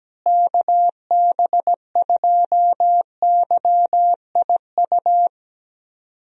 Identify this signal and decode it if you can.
Morse code, heard as a keyed tone, about 17 wpm, 700 Hz. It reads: KB2YIU
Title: USA Call Signs - 6 Characters - Repeated 3x - 17wpm